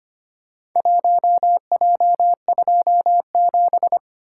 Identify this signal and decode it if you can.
Morse code, heard as a keyed tone, about 25 wpm, 700 Hz. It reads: 1J27
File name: practice / 25wpm / Four Letter-Number Combinations - 25wpm